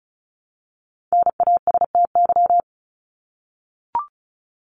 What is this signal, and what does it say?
Morse code, heard as a keyed tone, about 34 wpm, 700 Hz. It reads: NASTY
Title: Top 1751-2000 Words 35wpm